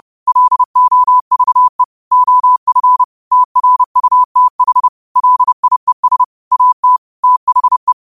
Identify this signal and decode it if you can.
Morse code, heard as a keyed tone, about 30 wpm, 1000 Hz. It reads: LOVE OF TRUTH LIES AT THE